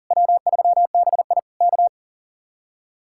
Morse code, heard as a keyed tone, about 50 wpm, 700 Hz. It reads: W3BI K
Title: Speed-Racing Common QSO Phrases - 40wpm